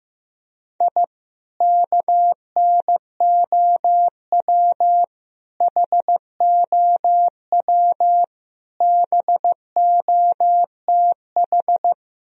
Morse code, heard as a keyed tone, about 16 wpm, 700 Hz. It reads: I KNOW HOW BOTH
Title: Sentences from Top 1000 Words - Review Part 1 - 15wpm